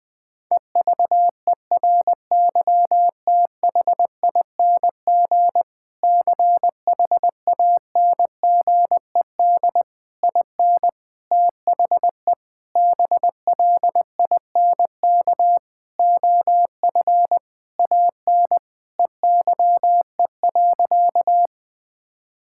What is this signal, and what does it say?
Morse code, heard as a keyed tone, about 20 wpm, 700 Hz. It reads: EVERYTHING CHANGED IN THE BLINK OF AN EYE.
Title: Sentences with 6-Word Phrases - 20wpm